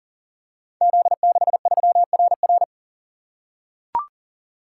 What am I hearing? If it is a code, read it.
Morse code, heard as a keyed tone, about 39 wpm, 700 Hz. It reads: Z63RR
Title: Code Groups - 5 Letter:Numbers - 40wpm